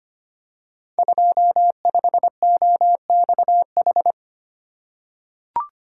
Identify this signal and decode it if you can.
Morse code, heard as a keyed tone, about 25 wpm, 700 Hz. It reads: 25OXH
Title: Code Groups - 5 Letter:Numbers - 25wpm